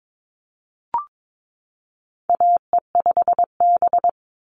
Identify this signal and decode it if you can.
Morse code, heard as a keyed tone, about 22 wpm, 700 Hz. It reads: AE5B
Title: NA SOTA Chaser Callsigns 22wpm